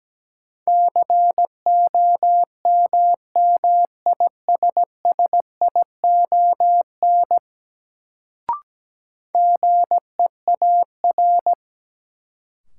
Morse code, heard as a keyed tone, about 17 wpm, 700 Hz. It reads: COMMISSION GEAR
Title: Sentences from Top 1400 Words - New Word Drill Alt - 17wpm